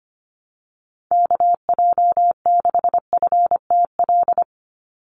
Morse code, heard as a keyed tone, about 25 wpm, 700 Hz. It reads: KJ6FTL
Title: USA Call Signs 25wpm